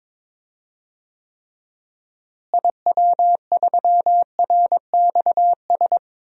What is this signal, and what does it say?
Morse code, heard as a keyed tone, about 22 wpm, 700 Hz. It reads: IW3RXS
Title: DXCC-100 Call Signs - 6 Characters - Repeated 3x - 22wpm